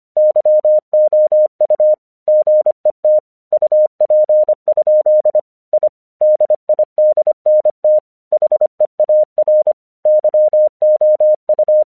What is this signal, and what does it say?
Morse code, heard as a keyed tone, about 26 wpm, 610 Hz. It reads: YOU GET UP? I DIDNT HEAR YOU
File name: 1 English 25wpm / AQC34